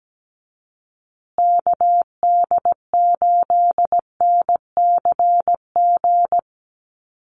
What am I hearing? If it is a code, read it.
Morse code, heard as a keyed tone, about 17 wpm, 700 Hz. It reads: KD8NCG
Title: USA Call Signs Repeated 3x 17wpm